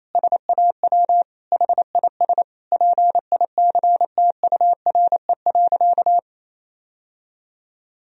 Morse code, heard as a keyed tone, about 28 wpm, 700 Hz. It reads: SAW HIS PICTURE.
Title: Sentences from Top 1000 Words - Review Part 1 - 28wpm